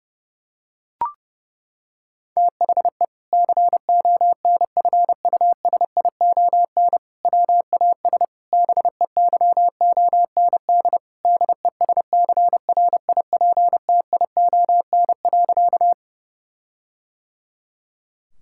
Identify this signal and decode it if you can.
Morse code, heard as a keyed tone, about 32 wpm, 700 Hz. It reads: THE CONFUSION WAS BEYOND DESCRIPTION.
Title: Sentences from Top 1600 Words 30wpm